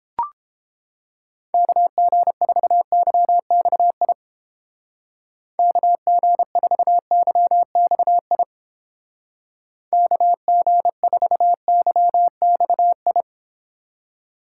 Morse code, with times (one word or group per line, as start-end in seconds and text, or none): 1.5-4.1 KG4YXI
5.6-8.4 KG4YXI
9.9-13.2 KG4YXI